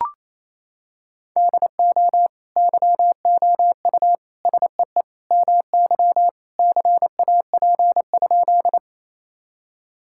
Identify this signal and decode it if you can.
Morse code, heard as a keyed tone, about 27 wpm, 700 Hz. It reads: DO YOU SEE MY CAP?
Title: Sentences from Top 1600 Words 28wpm